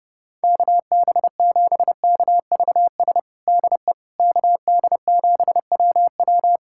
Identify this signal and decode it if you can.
Morse code, heard as a keyed tone, about 31 wpm, 700 Hz. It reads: KB7KVS DE KD7WW
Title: USA Callsign DE Callsign 30wpm